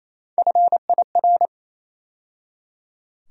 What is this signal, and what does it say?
Morse code, heard as a keyed tone, about 28 wpm, 700 Hz. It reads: FIR
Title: USA Call Sign Postfixes - 3 Letters 28wpm